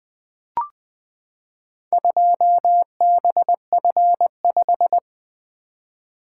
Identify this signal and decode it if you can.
Morse code, heard as a keyed tone, about 20 wpm, 700 Hz. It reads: 2BF5
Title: Four Letter-Number Combinations - 20wpm